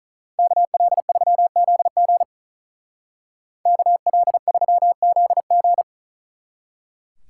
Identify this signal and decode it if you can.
Morse code, heard as a keyed tone, about 43 wpm, 700 Hz. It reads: KL3ZG KL3ZG
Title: Speed Racing - USA 2x2 Call Signs 35wpm